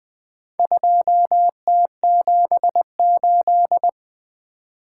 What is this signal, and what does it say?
Morse code, heard as a keyed tone, about 20 wpm, 700 Hz. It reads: 2T78
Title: Four Letter-Number Combinations - 20wpm